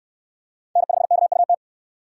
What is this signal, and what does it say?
Morse code, heard as a keyed tone, about 68 wpm, 700 Hz. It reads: N5KUT